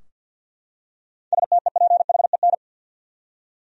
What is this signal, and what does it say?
Morse code, heard as a keyed tone, about 45 wpm, 700 Hz. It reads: STEPHEN
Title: USA Common Names - 50wpm